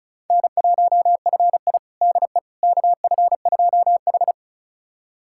Morse code, heard as a keyed tone, about 34 wpm, 700 Hz. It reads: N1FI DE KF2H
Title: USA Callsign DE Callsign - 4 Characters - 35wpm